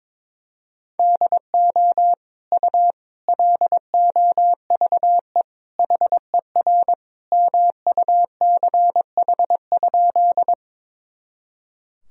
Morse code, heard as a keyed tone, about 22 wpm, 700 Hz. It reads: DO U LOVE HER MUCH?
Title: Sentences from Top 400 Words and Abbreviations 22wpm